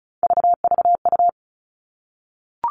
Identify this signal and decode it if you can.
Morse code, heard as a keyed tone, about 34 wpm, 700 Hz. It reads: VVU
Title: All Cut Numbers - 3-Digit 35wpm